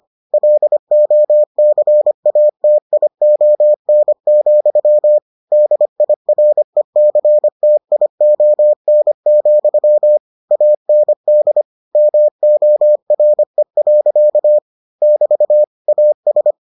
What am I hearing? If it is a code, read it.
Morse code, heard as a keyed tone, about 25 wpm, 600 Hz. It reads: LOCATION, DIRECTION, AND MORE. <BT> AS